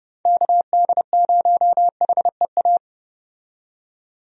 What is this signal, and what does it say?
Morse code, heard as a keyed tone, about 30 wpm, 700 Hz. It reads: KD0HEA
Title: USA Call Signs - 6 Characters - 30wpm